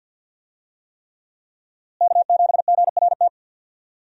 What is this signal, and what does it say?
Morse code, heard as a keyed tone, about 52 wpm, 700 Hz. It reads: K7GRT
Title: USA Call Signs - 5 Characters - Repeated 3x - 50wpm